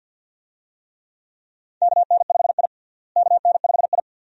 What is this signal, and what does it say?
Morse code, heard as a keyed tone, about 51 wpm, 700 Hz. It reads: KN5I KN5I
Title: USA Call Signs - 4 Characters - Repeated 2x - 50wpm